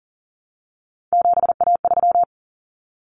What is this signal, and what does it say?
Morse code, heard as a keyed tone, about 40 wpm, 700 Hz. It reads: 7A3
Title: Common Cut Numbers - 3-Digit 40wpm